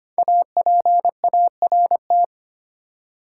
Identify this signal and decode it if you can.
Morse code, heard as a keyed tone, about 25 wpm, 700 Hz. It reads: APART
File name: 5 Letter Words - First 3 Letters Given - 25wpm